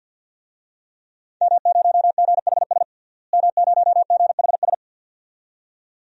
Morse code, heard as a keyed tone, about 50 wpm, 700 Hz. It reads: M0GHS M0GHS
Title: DXCC-100 Call Signs - 5 Characters - Repeated 2x - 50wpm